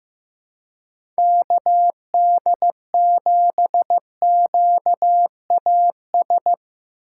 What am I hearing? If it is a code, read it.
Morse code, heard as a keyed tone, about 15 wpm, 700 Hz. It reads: KD7QAS
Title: USA Call Signs - 6 Characters - 3x Character Spacing - 15wpm